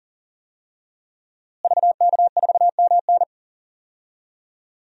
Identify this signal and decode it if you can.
Morse code, heard as a keyed tone, about 40 wpm, 700 Hz. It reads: VK4MN
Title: DXCC-100 Call Signs - 5 Characters - Repeated 2x - 40wpm